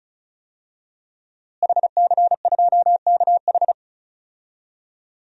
Morse code, heard as a keyed tone, about 34 wpm, 700 Hz. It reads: HC2KH
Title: DXCC-100 Call Signs - 5 Characters - Repeated 2x - 35wpm